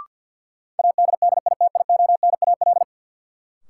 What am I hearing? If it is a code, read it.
Morse code, heard as a keyed tone, about 53 wpm, 700 Hz. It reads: ADDITIONAL
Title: Sentences from Top 1000 Words - Review Part 4 - New Word Drill - 50wpm